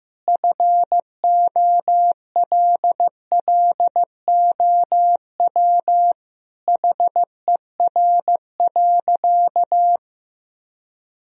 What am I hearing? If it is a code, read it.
Morse code, heard as a keyed tone, about 16 wpm, 700 Hz. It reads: FOLLOW HER.
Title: Sentences from Top 1000 Words - Review Part 1 - 15wpm